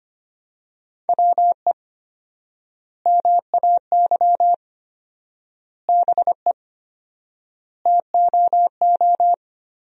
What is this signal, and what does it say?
Morse code, heard as a keyed tone, about 25 wpm, 700 Hz. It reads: WE MAY BE TOO
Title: 4x Word-Spacing - Sentences from Top 300 Words - 25wpm